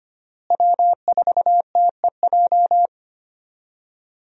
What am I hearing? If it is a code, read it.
Morse code, heard as a keyed tone, about 25 wpm, 700 Hz. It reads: W4TEJ
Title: USA Call Signs - 5 Characters - Repeated 3x - 25wpm